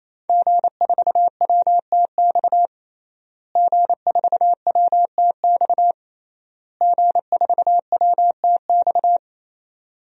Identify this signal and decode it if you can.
Morse code, heard as a keyed tone, about 28 wpm, 700 Hz. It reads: G4WTX G4WTX G4WTX
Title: DXCC-100 Call Signs - 5 Characters - Repeated 3x - 28wpm